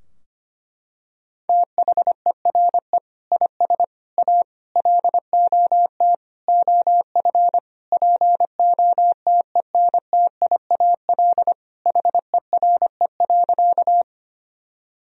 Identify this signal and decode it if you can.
Morse code, heard as a keyed tone, about 25 wpm, 700 Hz. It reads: THERE IS A LOT OF POTENTIAL HERE.
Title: Sentences from Top 1000 Words - Review Part 4 - 25wpm